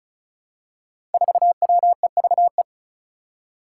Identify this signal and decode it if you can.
Morse code, heard as a keyed tone, about 36 wpm, 700 Hz. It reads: 4WEVE
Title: Five Challenging Letter-Number Combinations 35wpm